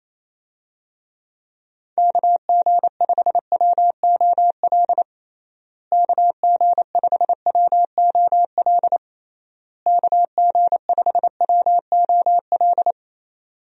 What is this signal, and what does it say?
Morse code, heard as a keyed tone, about 28 wpm, 700 Hz. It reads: KG5WOL KG5WOL KG5WOL
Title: USA Call Signs - 6 Characters - Repeated 3x - 28wpm